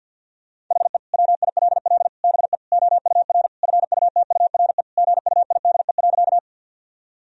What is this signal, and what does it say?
Morse code, heard as a keyed tone, about 49 wpm, 700 Hz. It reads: HE WILL BE OUR FUTURE GUIDE.
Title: Sentences from Top 1000 Words - Review Part 3 - 50wpm